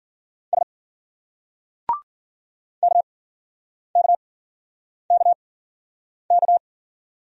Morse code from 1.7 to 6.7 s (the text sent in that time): KKKK